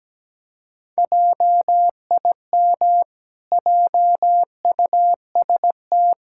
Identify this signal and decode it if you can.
Morse code, heard as a keyed tone, about 17 wpm, 700 Hz. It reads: JIM JUST
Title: Pangram Challenge v2 - 17wpm